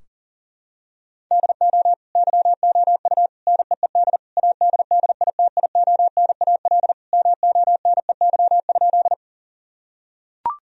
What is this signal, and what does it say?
Morse code, heard as a keyed tone, about 40 wpm, 700 Hz. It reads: DO YOU NEED ADDITIONAL MONEY?
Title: Sentences from Top 1000 Words - Review Part 4 - 40wpm